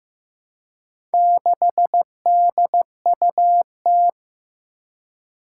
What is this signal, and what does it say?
Morse code, heard as a keyed tone, about 15 wpm, 700 Hz. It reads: 6DUT
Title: Four Challenging Letter-Number Combinations 15wpm